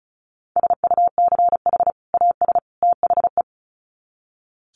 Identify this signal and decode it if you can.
Morse code, heard as a keyed tone, about 34 wpm, 700 Hz. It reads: SUCH AS THE
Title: Sets of 3 Words 35wpm